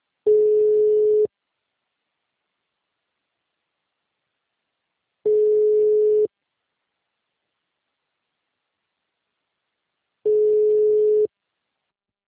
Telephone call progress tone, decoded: ringback tone

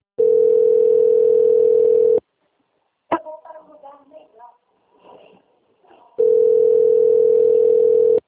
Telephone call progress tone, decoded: ringback tone

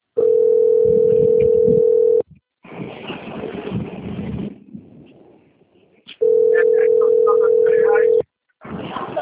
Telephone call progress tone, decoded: ringback tone

